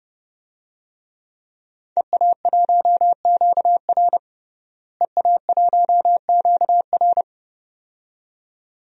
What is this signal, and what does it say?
Morse code, heard as a keyed tone, about 30 wpm, 700 Hz. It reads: EA1QR EA1QR
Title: DXCC-100 Call Signs - 5 Characters - Repeated 2x - 30wpm